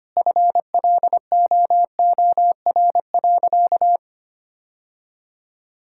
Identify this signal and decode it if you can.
Morse code, heard as a keyed tone, about 25 wpm, 700 Hz. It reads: FLOOR.
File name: practice / 25wpm / Sentences with 3-Word Phrases - 25wpm